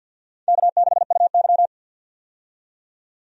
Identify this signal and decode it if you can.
Morse code, heard as a keyed tone, about 52 wpm, 700 Hz. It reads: K6UY
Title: USA Call Signs - 4 Characters - 50wpm